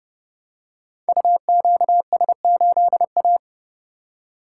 Morse code, heard as a keyed tone, about 30 wpm, 700 Hz. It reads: UQS8A
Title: Code Groups - 5 Letter:Numbers - 30wpm